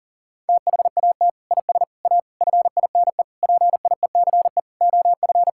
Morse code, heard as a keyed tone, about 40 wpm, 700 Hz. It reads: THAT IS A FINE PIECE OF
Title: Sentences from Top 1000 Words - Review Part 5 - 40wpm